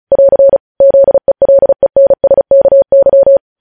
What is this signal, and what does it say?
Morse code, heard as a keyed tone, about 36 wpm, 550 Hz. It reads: <AR> ZELENSKY
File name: podcast / News headlines-1x-35x35